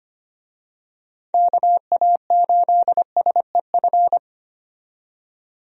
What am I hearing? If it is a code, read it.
Morse code, heard as a keyed tone, about 25 wpm, 700 Hz. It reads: KA8SEF
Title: USA Call Signs - 6 Characters - 25wpm